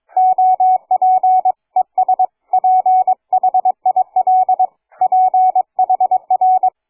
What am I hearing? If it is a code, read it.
Morse code, heard as a keyed tone, about 22 wpm, 740 Hz. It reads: OP ES PHIL PHR
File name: cw forum / fg5fr 160m 02h30